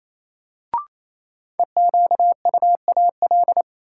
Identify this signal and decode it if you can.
Morse code, heard as a keyed tone, about 28 wpm, 700 Hz. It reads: EQUAL
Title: Sentences from Top 1400 Words - New Word Drill - 28wpm